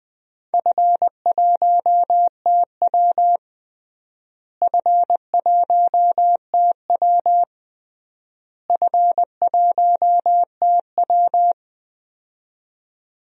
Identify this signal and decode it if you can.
Morse code, heard as a keyed tone, about 21 wpm, 700 Hz. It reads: F1TW F1TW F1TW
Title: DXCC-100 Call Signs - 4 Characters - Repeated 3x - 20wpm